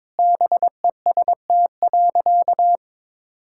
Morse code, heard as a keyed tone, about 22 wpm, 700 Hz. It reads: BEST.
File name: practice / 22wpm / Sentences with 2-Word Phrases - 22wpm